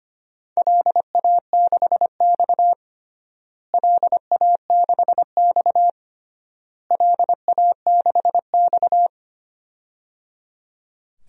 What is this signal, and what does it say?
Morse code, heard as a keyed tone, about 25 wpm, 700 Hz. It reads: LA6X LA6X LA6X
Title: DXCC-100 Call Signs - 4 Characters - Repeated 3x - 25wpm